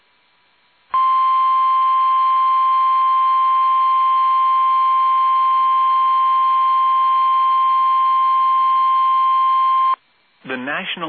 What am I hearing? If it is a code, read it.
Emergency Alert System (SAME) tones (the 1050 Hz attention tone), with speech. It is heard from 0.9 to 10.0 s.